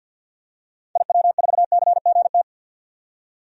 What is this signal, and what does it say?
Morse code, heard as a keyed tone, about 52 wpm, 700 Hz. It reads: IW4CGT
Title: DXCC-100 Call Signs - 6 Characters - 50wpm